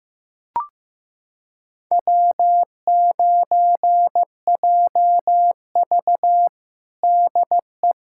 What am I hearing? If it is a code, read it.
Morse code, heard as a keyed tone, about 15 wpm, 700 Hz. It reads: W9JV DE
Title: USA Callsign DE Callsign - 4 Characters - 15wpm